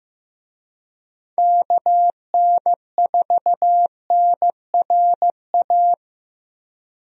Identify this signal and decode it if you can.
Morse code, heard as a keyed tone, about 15 wpm, 700 Hz. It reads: KN4NRA
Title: USA Call Signs - 6 Characters - Repeated 3x - 15wpm